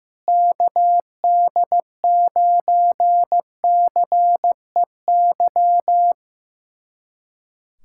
Morse code, heard as a keyed tone, about 16 wpm, 700 Hz. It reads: KD9CEY
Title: USA Callsign DE Callsign 15wpm